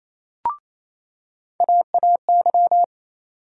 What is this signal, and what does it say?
Morse code, heard as a keyed tone, about 28 wpm, 700 Hz. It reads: AAY
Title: USA Call Sign Postfixes - 3 Letters 28wpm